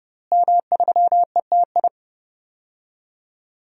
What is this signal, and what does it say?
Morse code, heard as a keyed tone, about 32 wpm, 700 Hz. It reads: M3ETI
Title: DXCC-100 Call Signs - 5 Characters - Repeated 3x - 30wpm